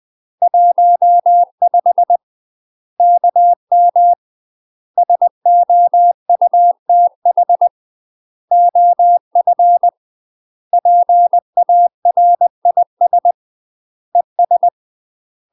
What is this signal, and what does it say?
Morse code, heard as a keyed tone, about 20 wpm, 700 Hz. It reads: 15 KM SOUTH OF PARIS ES
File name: UK MASTER QSOs TRAINING / 20WPM / UK MASTER QSOs 0035